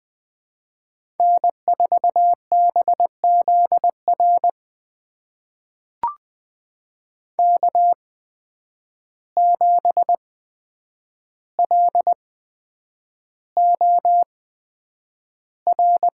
Morse code, heard as a keyed tone, about 20 wpm, spaced as for 9 wpm, 700 Hz. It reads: N4BZR K7LOR
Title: USA Call Signs - 5 Characters - 8x Character Spacing - 20wpm